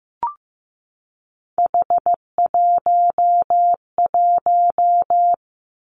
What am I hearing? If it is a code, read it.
Morse code, heard as a keyed tone, about 15 wpm, 700 Hz. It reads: H11
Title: Three Challenging Letter-Number Combinations 15wpm